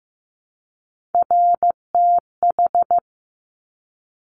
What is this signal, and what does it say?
Morse code, heard as a keyed tone, about 15 wpm, 700 Hz. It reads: RTH
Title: Top Three Letter Combinations in Words 15wpm